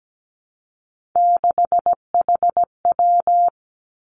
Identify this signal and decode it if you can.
Morse code, heard as a keyed tone, about 17 wpm, 700 Hz. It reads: 6HW
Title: Three Challenging Letter-Number Combinations 17wpm